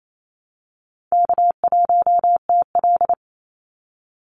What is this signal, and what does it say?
Morse code, heard as a keyed tone, about 28 wpm, 700 Hz. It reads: K1TL
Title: Easy Call Signs - 28wpm